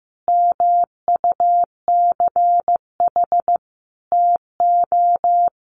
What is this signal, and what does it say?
Morse code, heard as a keyed tone, about 15 wpm, 700 Hz. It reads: MUCH TO
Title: Sentences from Top 1400 Words 15wpm